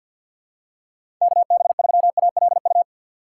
Morse code, heard as a keyed tone, about 52 wpm, 700 Hz. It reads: KB3ALU
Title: USA Call Signs - 6 Characters - 3x Character Spacing - 50wpm